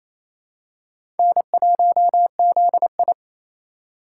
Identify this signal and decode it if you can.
Morse code, heard as a keyed tone, about 28 wpm, 700 Hz. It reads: N1ZI
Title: USA Call Signs - 4 Characters - 2x Character Spacing - 28wpm